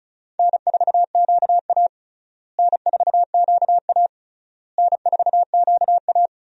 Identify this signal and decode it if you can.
Morse code, heard as a keyed tone, about 36 wpm, 700 Hz. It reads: N4QA N4QA N4QA
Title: USA Call Signs - 4 Characters - Repeated 3x - 35wpm